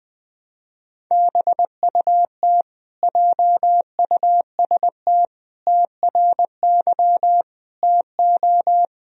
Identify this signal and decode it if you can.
Morse code, heard as a keyed tone, about 20 wpm, 700 Hz. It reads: BUT JUST TRY TO
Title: Sentences from Top 1000 Words - Review Part 1 - 20wpm